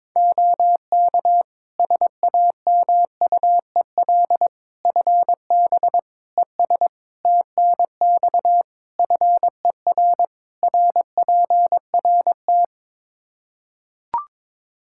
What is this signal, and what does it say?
Morse code, heard as a keyed tone, about 23 wpm, 700 Hz. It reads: OK SAMUEL FB ES TNX FER RPRT
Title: Practice QSO - Exchange 2 - Info - 22wpm